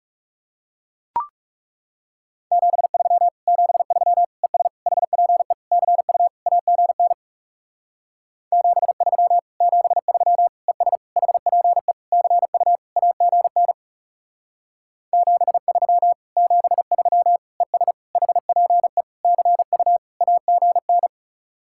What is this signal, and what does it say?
Morse code, heard as a keyed tone, about 39 wpm, 700 Hz. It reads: 73 73 ES HPE CU AGN 73 73 ES HPE CU AGN 73 73 ES HPE CU AGN